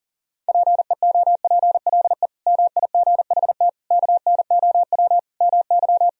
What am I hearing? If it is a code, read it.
Morse code, heard as a keyed tone, about 40 wpm, 700 Hz. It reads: PEOPLE MIGHT KNOW MY
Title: Sentences from Top 500 Words and Abbreviations 40wpm